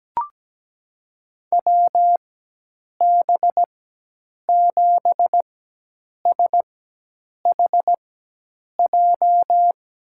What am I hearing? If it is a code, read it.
Morse code, heard as a keyed tone, about 17 wpm, spaced as for 8 wpm, 700 Hz. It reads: WB7SHJ
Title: USA Call Signs - 6 Characters - 4x Character Spacing - 17wpm